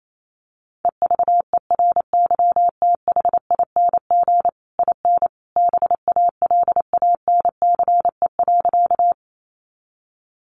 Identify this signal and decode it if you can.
Morse code, heard as a keyed tone, about 28 wpm, 700 Hz. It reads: EVERYTHING IN BALANCE.
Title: A Sound Of Thunder 28wpm